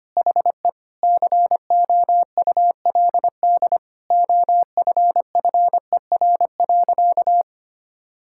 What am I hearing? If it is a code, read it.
Morse code, heard as a keyed tone, about 25 wpm, 700 Hz. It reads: HE COULD OFFER.
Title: Sentences from Top 1600 Words 25wpm